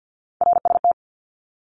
Morse code, heard as a keyed tone, about 50 wpm, 700 Hz. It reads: RST
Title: Abbreviations 50wpm